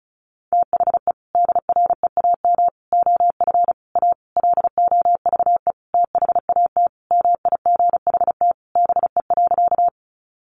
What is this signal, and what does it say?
Morse code, heard as a keyed tone, about 34 wpm, 700 Hz. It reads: THE DREAM OF A LOVE THAT MIGHT BE.
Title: Sentences from Top 1500 Words 35wpm